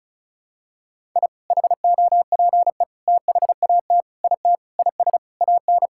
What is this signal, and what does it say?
Morse code, heard as a keyed tone, about 36 wpm, 700 Hz. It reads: I HOPE THAT IT IS AN
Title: Sentences from Top 1000 Words - Review Part 5 - 35wpm